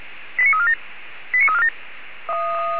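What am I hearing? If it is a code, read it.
Five-tone selective call (ZVEI), heard as quick runs of five tones, tones of about 70 ms. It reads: 89347
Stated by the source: ZVEI five-tone call received by SDR, recorded by Feuerwehr Orlamünde.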